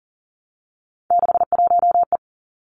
Morse code, heard as a keyed tone, about 42 wpm, 700 Hz. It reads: 61E
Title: Three Challenging Letter-Number Combinations 40wpm